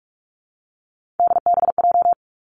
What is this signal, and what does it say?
Morse code, heard as a keyed tone, about 46 wpm, 700 Hz. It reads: DBJ
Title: Three Challenging Letter-Number Combinations 45wpm